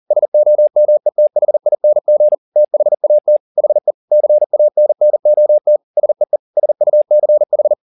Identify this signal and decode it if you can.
Morse code, heard as a keyed tone, about 40 wpm, 600 Hz. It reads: SOMETHING THAT HE CANNOT SEE SUCH